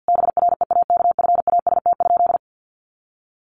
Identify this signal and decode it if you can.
Morse code, heard as a keyed tone, about 50 wpm, 700 Hz. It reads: BREAKFAST?
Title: Sentences from Top 1500 Words 50wpm